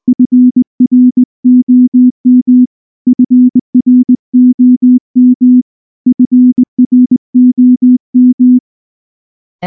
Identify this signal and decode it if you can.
Morse code, heard as a keyed tone, about 19 wpm, 250 Hz. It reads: FROM FROM FROM
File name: morse / morse-training-20WPM